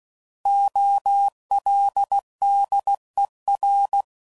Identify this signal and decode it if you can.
Morse code, heard as a keyed tone, about 16 wpm, 800 Hz. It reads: OLDER
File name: Prefixes / By First Letter / Os